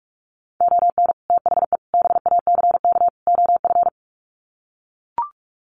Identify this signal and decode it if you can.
Morse code, heard as a keyed tone, about 44 wpm, 700 Hz. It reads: ON THE BACK OF
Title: Sets of 4 Words 45wpm